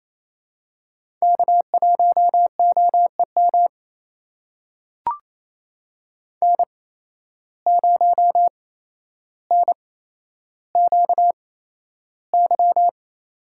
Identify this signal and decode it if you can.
Morse code, heard as a keyed tone, about 28 wpm, spaced as for 12 wpm, 700 Hz. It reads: K1OEM N0NQY
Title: USA Call Signs - 5 Characters - 8x Character Spacing - 28wpm